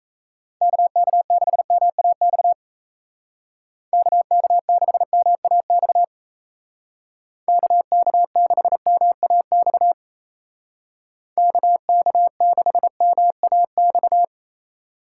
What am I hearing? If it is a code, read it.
Morse code, heard as a keyed tone, about 36 wpm, 700 Hz. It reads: KK6MAX KK6MAX KK6MAX KK6MAX